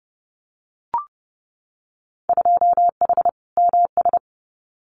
Morse code, heard as a keyed tone, about 30 wpm, 700 Hz. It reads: 2H MS
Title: Winter Field Day Exchanges 30wpm